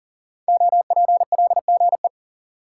Morse code, heard as a keyed tone, about 41 wpm, 700 Hz. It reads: OPLGE
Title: Code Groups - 5 Letters - 40wpm